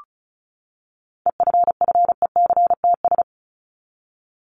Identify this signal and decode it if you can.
Morse code, heard as a keyed tone, about 34 wpm, 700 Hz. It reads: EFFECTS
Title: ES:S Suffix - Top 100 Words with Base Words 35wpm